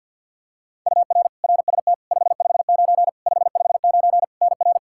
Morse code, heard as a keyed tone, about 47 wpm, 700 Hz. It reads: UR RST 559 559 NR
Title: Practice QSO - Exchange 1 - Beginning - 50wpm